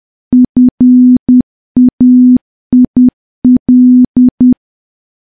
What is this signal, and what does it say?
Morse code, heard as a keyed tone, about 10 wpm, 250 Hz. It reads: FAIL
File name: morse / morse-training-10WPM